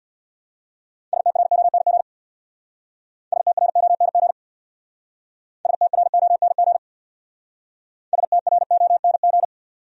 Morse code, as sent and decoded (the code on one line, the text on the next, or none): ... - .-. --- -. --. / ... - .-. --- -. --. / ... - .-. --- -. --. / ... - .-. --- -. --.
STRONG STRONG STRONG STRONG